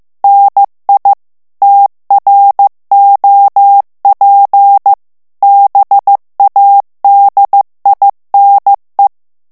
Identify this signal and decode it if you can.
Morse code, heard as a keyed tone, about 15 wpm, 800 Hz. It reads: NI TROP BADINE
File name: morse mp3 / FRANCAIS TEXTE CLAIR 2 VITESSE 14 MOTS MINUTES